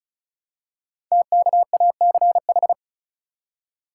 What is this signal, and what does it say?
Morse code, heard as a keyed tone, about 36 wpm, 700 Hz. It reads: TKACH
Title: Code Groups - 5 Letters - 35wpm